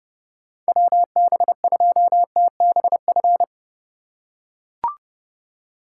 Morse code, heard as a keyed tone, about 30 wpm, 700 Hz. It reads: WB2TBF
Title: USA Call Signs - 3x Character Spacing - 30wpm